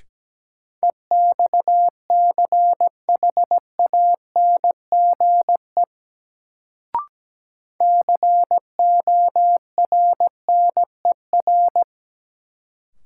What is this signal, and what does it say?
Morse code, heard as a keyed tone, about 17 wpm, 700 Hz. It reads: EXCHANGE CORNER
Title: Sets of 5 Words - New Word Drill - 17wpm